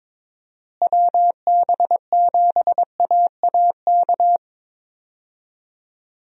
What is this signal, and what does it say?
Morse code, heard as a keyed tone, about 22 wpm, 700 Hz. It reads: WB7AAK
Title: USA Call Signs - 6 Characters - Repeated 2x - 22wpm